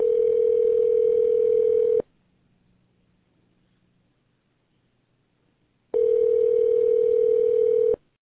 Ringback tone (North America), 440 + 480 Hz.